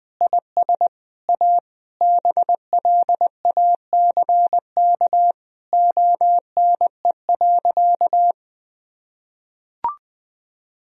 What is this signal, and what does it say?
Morse code, heard as a keyed tone, about 20 wpm, 700 Hz. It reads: IS A BLACK ONE.
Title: Sentences from Top 1000 Words - Review Part 2 - 20wpm